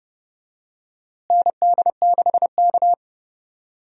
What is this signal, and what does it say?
Morse code, heard as a keyed tone, about 30 wpm, 700 Hz. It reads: ND6K
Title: USA Call Signs - 4 Characters - Repeated 3x - 30wpm